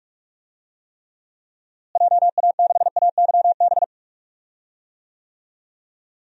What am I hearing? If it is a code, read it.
Morse code, heard as a keyed tone, about 46 wpm, 700 Hz. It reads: JA6AYB
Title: DXCC-100 Call Signs - 6 Characters - Repeated 2x - 45wpm